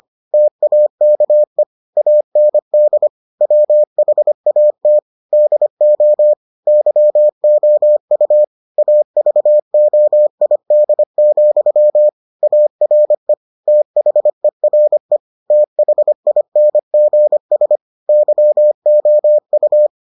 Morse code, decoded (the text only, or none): TAKE AND WHAT DO YOU AVOID, ARE THERE THINGS YOU